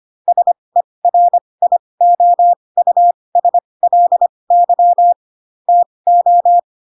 Morse code, heard as a keyed tone, about 25 wpm, spaced as for 20 wpm, 700 Hz. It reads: SERIOUSLY TO
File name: Pinocchio / Pinocchio ch3 15 25 ik1hge